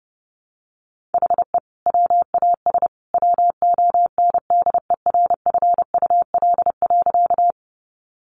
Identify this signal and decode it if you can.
Morse code, heard as a keyed tone, about 29 wpm, 700 Hz. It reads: HE WAS WONDERFUL.
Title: Sentences from Top 1100 Words 30wpm